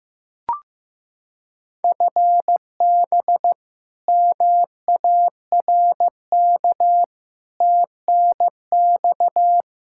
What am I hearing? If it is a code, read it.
Morse code, heard as a keyed tone, about 15 wpm, 700 Hz. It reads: FB MARK TNX
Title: K1USN SST QSO Practice 15wpm